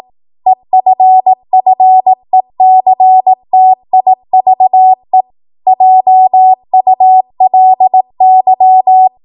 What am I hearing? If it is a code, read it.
Morse code, heard as a keyed tone, about 18 wpm, 750 Hz. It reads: EFFECTIVE JULY
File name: Archive / 18 WPM / 18WPM